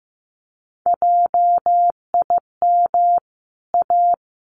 Morse code, heard as a keyed tone, about 15 wpm, 700 Hz. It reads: JIM A
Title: CWT Exchange 15wpm